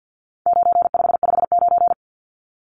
Morse code, heard as a keyed tone, about 50 wpm, 700 Hz. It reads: 9559
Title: Numbers - 4-Digit 50wpm